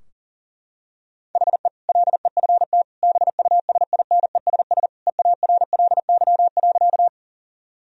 Morse code, heard as a keyed tone, about 37 wpm, 700 Hz. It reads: HE LEFT BUSINESS EARLY.
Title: Sentences from Top 1000 Words - Review Part 2 - 40wpm